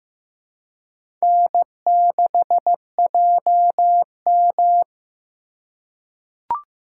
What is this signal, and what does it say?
Morse code, heard as a keyed tone, about 15 wpm, 700 Hz. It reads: N6JM
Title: USA Call Signs - 4 Characters - 2x Character Spacing - 15wpm